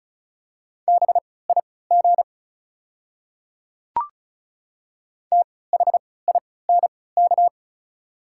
Morse code, heard as a keyed tone, about 36 wpm, spaced as for 20 wpm, 690 Hz. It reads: BIG THINK
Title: 3x Character Spacing - Top 200 Words - 35wpm